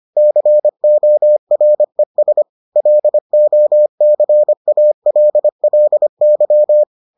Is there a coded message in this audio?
Yes: Morse code from 0.2 to 6.8 s, heard as a keyed tone, about 25 wpm, 600 Hz. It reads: CORES LOCALLY